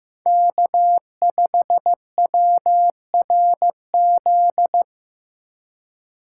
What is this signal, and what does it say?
Morse code, heard as a keyed tone, about 15 wpm, 700 Hz. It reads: K5WRZ
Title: USA Call Signs - 5 Characters - 2x Character Spacing - 15wpm